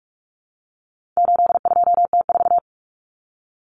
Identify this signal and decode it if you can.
Morse code, heard as a keyed tone, about 44 wpm, 700 Hz. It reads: 82T4
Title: Common Cut Numbers - 4-Digit 45wpm